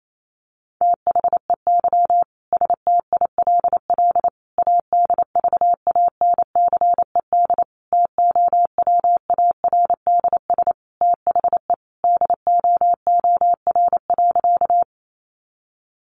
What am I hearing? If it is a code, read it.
Morse code, heard as a keyed tone, about 28 wpm, 700 Hz. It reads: THEY STILL ADVANCED TOWARDS THE DOOR.
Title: Sentences from Top 1200 Words 28wpm